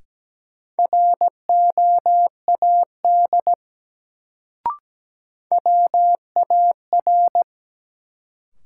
Morse code, heard as a keyed tone, about 17 wpm, 700 Hz. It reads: ROAD WAR
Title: Sets of 5 Words - New Word Drill - 17wpm